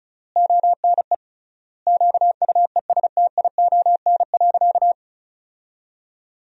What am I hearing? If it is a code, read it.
Morse code, heard as a keyed tone, about 34 wpm, 700 Hz. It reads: ONE QUESTION.
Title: 3x Word-Spacing - Sentences from Top 400 Words - 35wpm